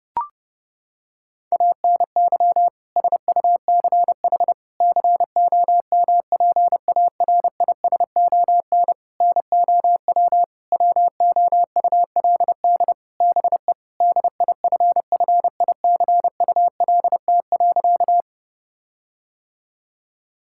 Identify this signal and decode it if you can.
Morse code, heard as a keyed tone, about 30 wpm, 700 Hz. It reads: ANY SUCH COMPARISON NOW WOULD BE DIFFICULT.
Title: Sentences from Top 1600 Words 30wpm